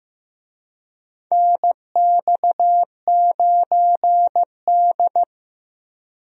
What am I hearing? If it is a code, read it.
Morse code, heard as a keyed tone, about 15 wpm, 700 Hz. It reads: NX9D
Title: USA Call Signs - 4 Characters - Repeated 2x - 15wpm